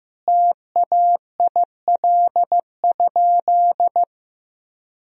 Morse code, heard as a keyed tone, about 15 wpm, 700 Hz. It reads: TAIL?
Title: Sentences from Top 1000 Words - Review Part 3 - 15wpm